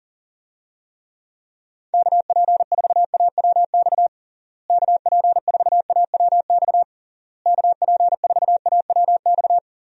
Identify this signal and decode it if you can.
Morse code, heard as a keyed tone, about 40 wpm, 700 Hz. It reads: KP4AWX KP4AWX KP4AWX
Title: DXCC-100 Call Signs - 6 Characters - Repeated 3x - 40wpm